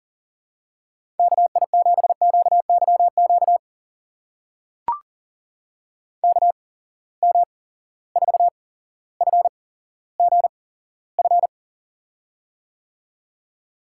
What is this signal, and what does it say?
Morse code, heard as a keyed tone, about 39 wpm, spaced as for 17 wpm, 700 Hz. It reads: KI7QYQ KM4FGF
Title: USA Call Signs - 6 Characters - 8x Character Spacing - 40wpm